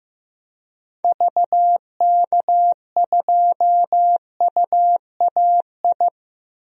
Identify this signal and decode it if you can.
Morse code, heard as a keyed tone, about 15 wpm, 700 Hz. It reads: VK2UAI
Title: DXCC-100 Call Signs - 6 Characters - 15wpm